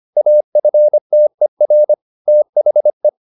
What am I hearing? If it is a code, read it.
Morse code, heard as a keyed tone, about 25 wpm, 600 Hz. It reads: AFTER THE
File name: the old man and the sea part 1 25wpm